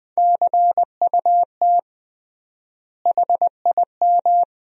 Morse code, heard as a keyed tone, about 20 wpm, 700 Hz. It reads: CUT HIM
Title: 3x Word-Spacing - Sentences from Top 300 Words - 20wpm